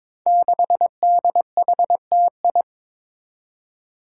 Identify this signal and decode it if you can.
Morse code, heard as a keyed tone, about 22 wpm, 700 Hz. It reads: 6DHTI
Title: Five Challenging Letter-Number Combinations 22wpm